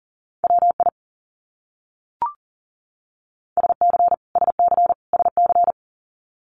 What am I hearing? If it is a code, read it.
Morse code, heard as a keyed tone, about 40 wpm, 700 Hz. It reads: WI SC SC SC
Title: US States Repeated 3x 40wpm